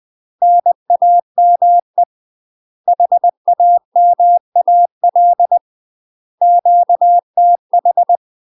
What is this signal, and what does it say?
Morse code, heard as a keyed tone, about 20 wpm, 700 Hz. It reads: NAME HAMAL QTH
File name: UK MASTER QSOs TRAINING / 20WPM / UK MASTER QSOs 0028